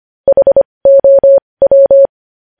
Morse code, heard as a keyed tone, about 25 wpm, spaced as for 20 wpm, 550 Hz. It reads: HOW
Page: News Headlines in Morse Code at 20 WPM